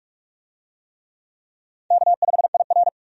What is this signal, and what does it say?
Morse code, heard as a keyed tone, about 43 wpm, 700 Hz. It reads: K5IR
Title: USA Call Signs - 4 Characters - 45wpm